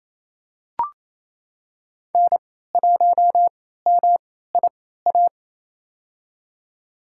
Morse code, heard as a keyed tone, about 27 wpm, spaced as for 16 wpm, 700 Hz. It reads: N1MIA
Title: USA Call Signs - 3x Character Spacing - 28wpm